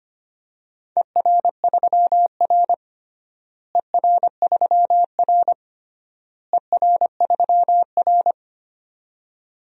Morse code, heard as a keyed tone, about 25 wpm, 700 Hz. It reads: ER3R ER3R ER3R
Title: DXCC-100 Call Signs - 4 Characters - Repeated 3x - 25wpm